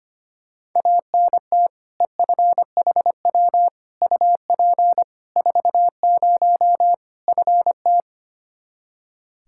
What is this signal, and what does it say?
Morse code, heard as a keyed tone, about 25 wpm, 700 Hz. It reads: ANT EFHW UP 40 FT
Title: K1USN SST QSO Practice 25wpm